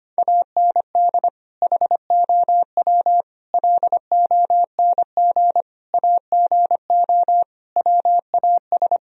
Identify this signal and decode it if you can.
Morse code, heard as a keyed tone, about 25 wpm, 700 Hz. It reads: AND HOW LONG AGO WAS
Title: Sentences from Top 1000 Words - Review Part 3 - 25wpm